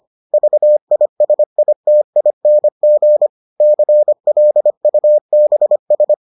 Morse code, heard as a keyed tone, about 25 wpm, 600 Hz. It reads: VISITING CLUBS